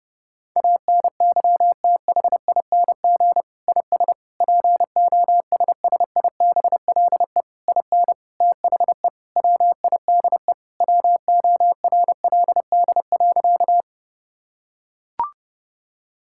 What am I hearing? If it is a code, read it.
Morse code, heard as a keyed tone, about 32 wpm, 700 Hz. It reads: ANYTHING IS POSSIBLE IN THE WIDE WORLD.
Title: Sentences from Top 1000 Words - Review Part 6 - 30wpm